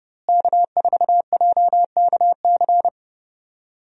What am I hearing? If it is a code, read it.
Morse code, heard as a keyed tone, about 30 wpm, 700 Hz. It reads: K4JKC
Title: USA Call Signs - 5 Characters - Repeated 3x - 30wpm